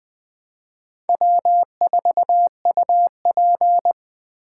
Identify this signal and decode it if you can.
Morse code, heard as a keyed tone, about 24 wpm, 700 Hz. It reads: W4UP
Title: Speed Racing - USA 1x2 Call Signs 20wpm